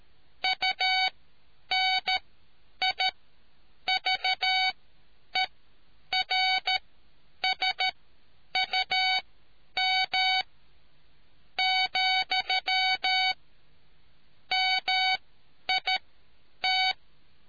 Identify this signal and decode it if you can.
Morse code, heard as a keyed tone, about 13 wpm, spaced as for 9 wpm, 750 Hz. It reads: UNIVERSUM , MIT